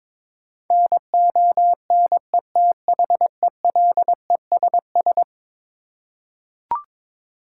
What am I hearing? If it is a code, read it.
Morse code, heard as a keyed tone, about 22 wpm, 700 Hz. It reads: NONETHELESS
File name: practice / 22wpm / Eleven Letter Words - 22wpm